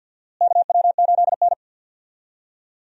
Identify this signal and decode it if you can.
Morse code, heard as a keyed tone, about 46 wpm, 700 Hz. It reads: KW8N
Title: USA Call Signs - 4 Characters - Repeated 2x - 50wpm